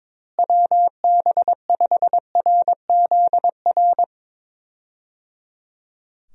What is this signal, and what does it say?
Morse code, heard as a keyed tone, about 22 wpm, 700 Hz. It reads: WB5RZR
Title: USA Call Signs - 6 Characters - Repeated 3x - 22wpm